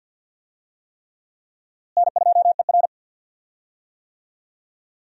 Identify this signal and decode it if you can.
Morse code, heard as a keyed tone, about 52 wpm, 700 Hz. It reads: N2ER
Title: USA Call Signs - 4 Characters - Repeated 3x - 50wpm